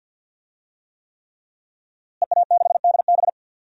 Morse code, heard as a keyed tone, about 47 wpm, 700 Hz. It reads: EA6DB
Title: DXCC-100 Call Signs - 5 Characters - Repeated 3x - 50wpm